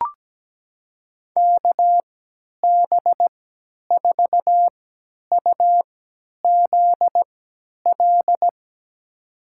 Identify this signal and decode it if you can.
Morse code, heard as a keyed tone, about 17 wpm, spaced as for 10 wpm, 700 Hz. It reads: KB4UZL
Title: USA Call Signs - 3x Character Spacing - 17wpm